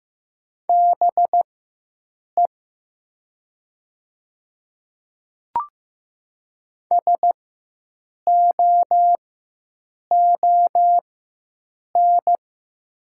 Morse code, heard as a keyed tone, about 15 wpm, spaced as for 7 wpm, 700 Hz. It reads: BE SOON
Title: 4x Character Spacing - Top 500 Words - 15wpm